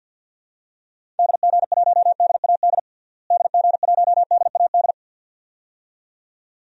Morse code, heard as a keyed tone, about 53 wpm, 700 Hz. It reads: DG1DAD DG1DAD
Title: DXCC-100 Call Signs - 6 Characters - Repeated 2x - 50wpm